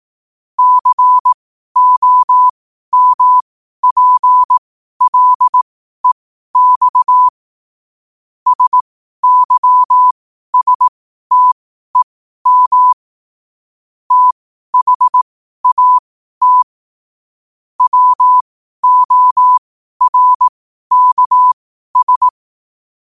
Morse code, heard as a keyed tone, about 18 wpm, spaced as for 12 wpm, 1000 Hz. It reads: COMPLEX SYSTEM THAT WORKS
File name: qotdfiles / 12wpm